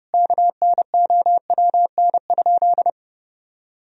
Morse code, heard as a keyed tone, about 30 wpm, 700 Hz. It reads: KNOWN?
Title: Sentences from Top 1000 Words - Review Part 3 - 30wpm